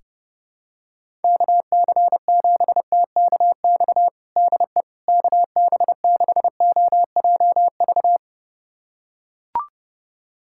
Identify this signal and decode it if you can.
Morse code, heard as a keyed tone, about 30 wpm, 700 Hz. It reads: KC7TKX DE KB6OJV
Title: USA Callsign DE Callsign - 6 Characters - 30wpm